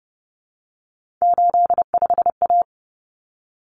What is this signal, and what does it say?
Morse code, heard as a keyed tone, about 30 wpm, 700 Hz. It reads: 85A
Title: Common Cut Numbers - 3-Digit 30wpm